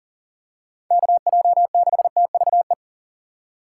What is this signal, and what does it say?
Morse code, heard as a keyed tone, about 40 wpm, 700 Hz. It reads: KJ6TVE
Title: USA Call Signs - 6 Characters - 2x Character Spacing - 40wpm